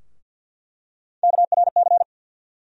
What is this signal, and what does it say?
Morse code, heard as a keyed tone, about 47 wpm, 700 Hz. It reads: KRC
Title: USA Call Sign Postfixes - 3 Letters 50wpm